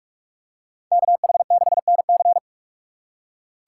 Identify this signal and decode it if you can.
Morse code, heard as a keyed tone, about 44 wpm, 700 Hz. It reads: KH6NC